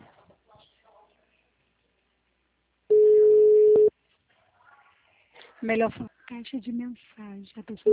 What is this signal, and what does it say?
Ringback tone (Europe, 425 Hz), with speech.
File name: out- -2013- - - .77302